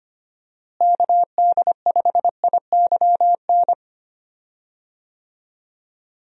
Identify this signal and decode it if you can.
Morse code, heard as a keyed tone, about 25 wpm, 700 Hz. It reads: KD5IYN
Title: USA Call Signs - 6 Characters - Repeated 3x - 25wpm